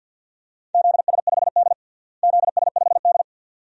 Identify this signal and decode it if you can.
Morse code, heard as a keyed tone, about 53 wpm, 700 Hz. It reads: ZS5D ZS5D
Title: DXCC-100 Call Signs - 4 Characters - Repeated 2x - 50wpm